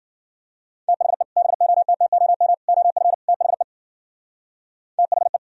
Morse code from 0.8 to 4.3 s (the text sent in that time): THE BOTTOM OF THE